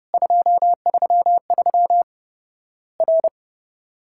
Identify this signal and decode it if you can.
Morse code, heard as a keyed tone, about 30 wpm, 700 Hz. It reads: 233 R
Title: Worked All Europe DX Contest - QTC Exchange - 30wpm